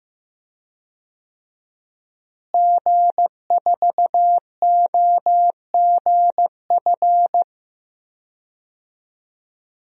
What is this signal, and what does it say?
Morse code, heard as a keyed tone, about 15 wpm, 700 Hz. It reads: G4OGF
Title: DXCC-100 Call Signs - 5 Characters - Repeated 2x - 15wpm